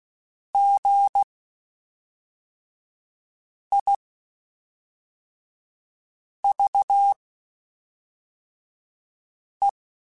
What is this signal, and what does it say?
Morse code, heard as a keyed tone, about 16 wpm, spaced as for 3 wpm, 800 Hz. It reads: GIVE
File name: Quotes Spaced / Quotes Spaced-mp3s / Advice sp @5